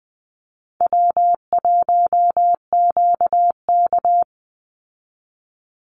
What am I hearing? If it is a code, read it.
Morse code, heard as a keyed tone, about 20 wpm, 700 Hz. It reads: W1QK
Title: Easy Call Signs - 20wpm